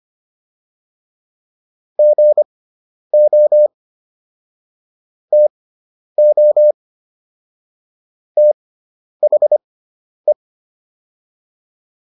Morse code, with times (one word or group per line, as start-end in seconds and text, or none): 2.0-3.7 GO
5.3-6.7 TO
8.4-10.3 THE